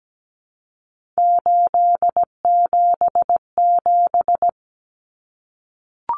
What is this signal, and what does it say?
Morse code, heard as a keyed tone, about 17 wpm, 700 Hz. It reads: 877
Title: Common Cut Numbers - 3-Digit 17wpm